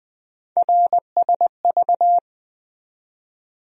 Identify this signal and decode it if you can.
Morse code, heard as a keyed tone, about 20 wpm, 700 Hz. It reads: RSV
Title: USA Call Sign Postfixes - 3 Letters 20wpm